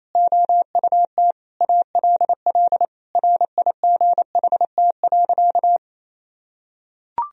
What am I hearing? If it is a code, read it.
Morse code, heard as a keyed tone, about 28 wpm, 700 Hz. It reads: OUT ALL RIGHT.
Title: Sentences from Top 1000 Words - Review Part 4 - 28wpm